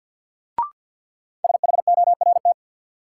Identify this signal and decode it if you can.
Morse code, heard as a keyed tone, about 53 wpm, 700 Hz. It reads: SHORT
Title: Sentences from Top 400 Words - New Word Drill - 50wpm